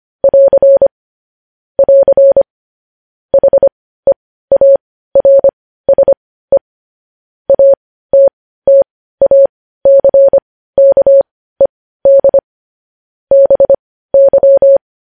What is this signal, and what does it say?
Morse code, heard as a keyed tone, about 25 wpm, spaced as for 15 wpm, 550 Hz. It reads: <AR> <AR> HEARSE ATTACKED BY
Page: news-headlines-in-morse-code-at-15-wpm